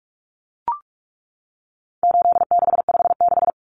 Morse code, heard as a keyed tone, about 44 wpm, 700 Hz. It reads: 8656
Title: Common Cut Numbers - 4-Digit 45wpm